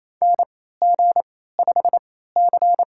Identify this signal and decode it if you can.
Morse code, heard as a keyed tone, about 28 wpm, spaced as for 16 wpm, 700 Hz. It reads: NG5C
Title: USA Call Signs - 4 Characters - 3x Character Spacing - 28wpm